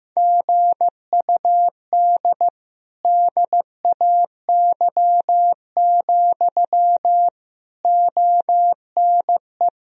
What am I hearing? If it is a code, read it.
Morse code, heard as a keyed tone, about 15 wpm, 700 Hz. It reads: GUD DAY, ONE
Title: Sentences from Top 200 Words and Abbreviations 15wpm